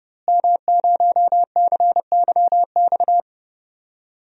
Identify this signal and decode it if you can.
Morse code, heard as a keyed tone, about 30 wpm, 700 Hz. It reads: M0CYX
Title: DXCC-100 Call Signs - 5 Characters - Repeated 3x - 30wpm